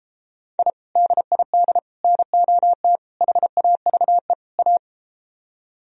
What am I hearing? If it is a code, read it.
Morse code, heard as a keyed tone, about 33 wpm, 700 Hz. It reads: I DID NOT HAVE A